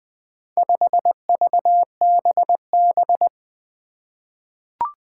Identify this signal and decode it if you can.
Morse code, heard as a keyed tone, about 21 wpm, 700 Hz. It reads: 5VBB
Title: Four Challenging Letter-Number Combinations 20wpm